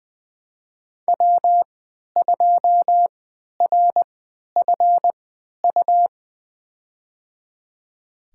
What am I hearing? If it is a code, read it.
Morse code, heard as a keyed tone, about 20 wpm, spaced as for 11 wpm, 700 Hz. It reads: W2RFU
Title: USA Call Signs - 5 Characters - 3x Character Spacing - 20wpm